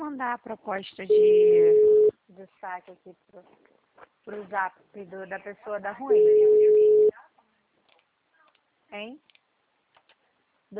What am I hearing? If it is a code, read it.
Ringback tone (Europe, 425 Hz), with speech.